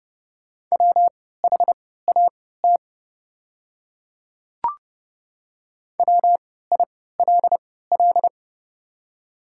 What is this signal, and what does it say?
Morse code, heard as a keyed tone, about 29 wpm, spaced as for 17 wpm, 700 Hz. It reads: WHAT WILL
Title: 3x Character Spacing - Top 200 Words - 30wpm